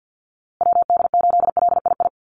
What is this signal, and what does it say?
Morse code, heard as a keyed tone, about 50 wpm, 700 Hz. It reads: WD8LII
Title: USA Call Signs Repeated 2x 50wpm